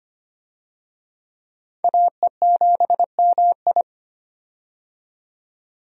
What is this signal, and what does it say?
Morse code, heard as a keyed tone, about 25 wpm, 700 Hz. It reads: AE7MI
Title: USA Call Signs - 5 Characters - 25wpm